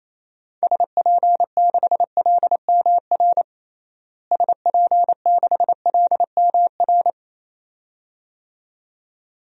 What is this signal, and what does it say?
Morse code, heard as a keyed tone, about 28 wpm, 700 Hz. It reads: SP6LMR SP6LMR
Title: DXCC-100 Call Signs - 6 Characters - Repeated 2x - 28wpm